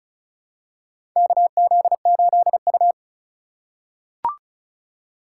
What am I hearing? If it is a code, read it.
Morse code, heard as a keyed tone, about 34 wpm, 700 Hz. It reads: KZ8U
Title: USA Call Signs - 4 Characters - 4x Character Spacing - 35wpm